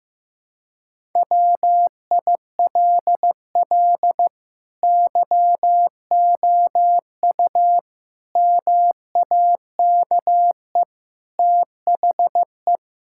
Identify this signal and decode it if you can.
Morse code, heard as a keyed tone, about 15 wpm, 700 Hz. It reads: WILL YOU MAKE THE
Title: Sentences from Top 1000 Words - Review Part 6 - 15wpm